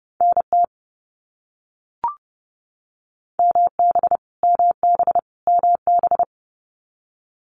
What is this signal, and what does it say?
Morse code, heard as a keyed tone, about 30 wpm, 700 Hz. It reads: NT MB MB MB
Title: Canadian Provinces and Territories - Repeated 3x - 30wpm